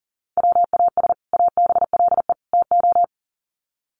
Morse code, heard as a keyed tone, about 40 wpm, 700 Hz. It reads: WAS ABLE TO
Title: Sets of 3 Words 40wpm